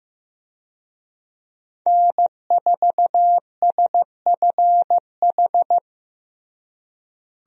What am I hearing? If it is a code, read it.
Morse code, heard as a keyed tone, about 15 wpm, 700 Hz. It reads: N4SFH
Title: USA Call Signs - 5 Characters - Repeated 2x - 15wpm